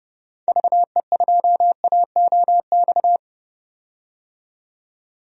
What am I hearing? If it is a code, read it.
Morse code, heard as a keyed tone, about 30 wpm, 700 Hz. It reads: VE2AOX
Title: DXCC-100 Call Signs - 6 Characters - 30wpm